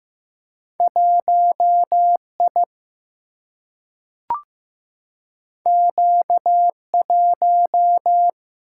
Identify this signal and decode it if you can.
Morse code, heard as a keyed tone, about 15 wpm, 700 Hz. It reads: 1I Q1
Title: Single-Letter Single-Number 15wpm